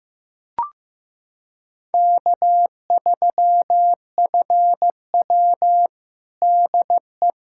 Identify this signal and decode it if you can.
Morse code, heard as a keyed tone, about 15 wpm, 700 Hz. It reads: K3FW DE
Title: USA Callsign DE Callsign - 4 Characters - 15wpm